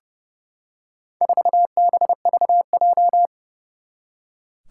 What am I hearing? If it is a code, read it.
Morse code, heard as a keyed tone, about 30 wpm, 700 Hz. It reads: 4BVJ
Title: Four Challenging Letter-Number Combinations 30wpm